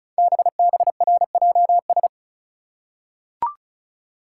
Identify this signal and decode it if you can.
Morse code, heard as a keyed tone, about 36 wpm, 700 Hz. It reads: BBRJS
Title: Code Groups - 5 Letters - 35wpm